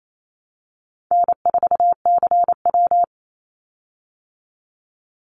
Morse code, heard as a keyed tone, about 28 wpm, 700 Hz. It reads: N4CW
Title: Call Signs Repeated 2x - 28wpm